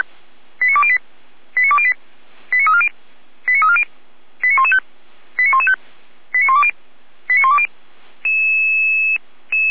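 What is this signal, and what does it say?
Five-tone selective call (ZVEI), heard as quick runs of five tones, tones of about 70 ms. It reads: 89208 89208 89340 89340 89115 89115 89120 89120
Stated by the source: ZVEI five-tone call 89094 received by SDR.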